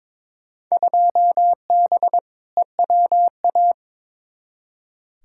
Morse code, heard as a keyed tone, about 22 wpm, 700 Hz. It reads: 2B EWA
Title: Field Day Exchanges Encore - 22wpm